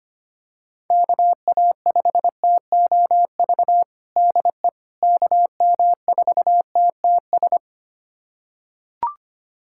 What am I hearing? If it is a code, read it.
Morse code, heard as a keyed tone, about 25 wpm, 700 Hz. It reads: KA5TOV DE KM4TTS
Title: USA Callsign DE Callsign - 6 Characters - 25wpm